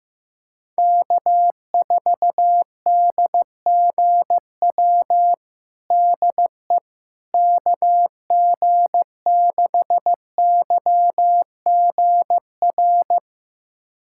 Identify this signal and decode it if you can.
Morse code, heard as a keyed tone, about 16 wpm, 700 Hz. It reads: K4DGW DE KG6YGR
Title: USA Callsign DE Callsign 15wpm